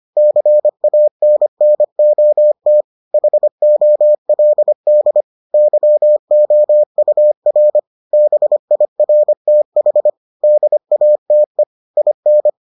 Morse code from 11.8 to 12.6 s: IN

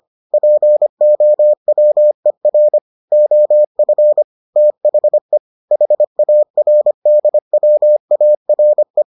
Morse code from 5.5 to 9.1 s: HARDWARE

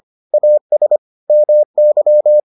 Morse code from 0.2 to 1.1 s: AS

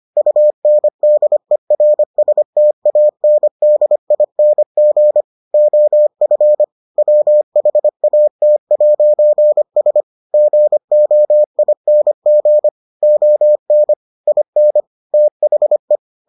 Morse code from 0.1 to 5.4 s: UNDERSTANDING

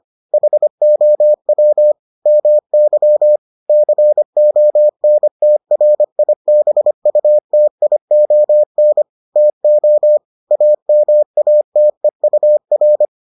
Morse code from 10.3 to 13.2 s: AMATEUR

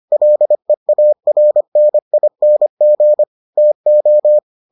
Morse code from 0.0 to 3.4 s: LEARNING

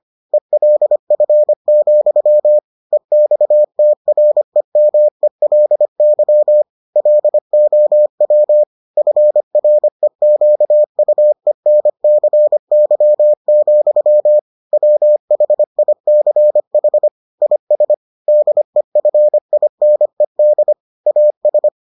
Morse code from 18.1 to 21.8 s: DEFINED AS

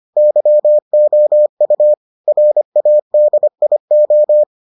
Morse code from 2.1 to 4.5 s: RADIO

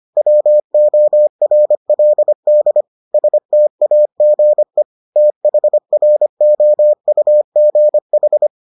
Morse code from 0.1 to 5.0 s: WORLD STAGE

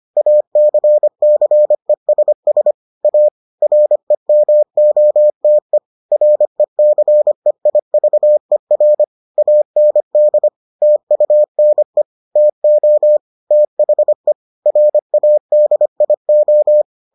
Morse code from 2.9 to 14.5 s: A REMOTE RECEIVER AND TUNE TO THE

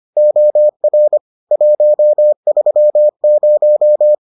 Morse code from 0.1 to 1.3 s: OR

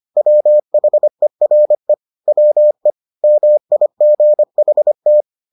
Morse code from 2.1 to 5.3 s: WE MIGHT

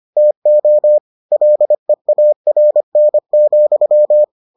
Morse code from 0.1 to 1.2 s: TO

